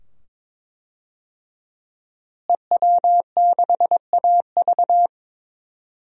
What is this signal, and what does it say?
Morse code, heard as a keyed tone, about 21 wpm, 700 Hz. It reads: EW6AV
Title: DXCC-100 Call Signs - 5 Characters - Repeated 2x - 22wpm